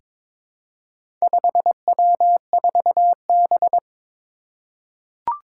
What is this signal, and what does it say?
Morse code, heard as a keyed tone, about 22 wpm, 700 Hz. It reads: 5W4B
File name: practice / 22wpm / Four Letter-Number Combinations - 22wpm